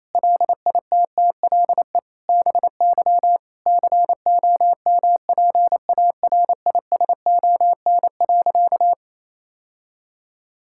Morse code, heard as a keyed tone, about 28 wpm, 700 Hz. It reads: LITTLE BY COMPARISON.
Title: Sentences from Top 1600 Words 28wpm